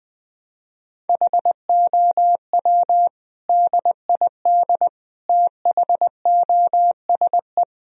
Morse code, heard as a keyed tone, about 20 wpm, 700 Hz. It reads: HOW DID THOSE
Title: Sentences from Top 1000 Words - Review Part 1 - 20wpm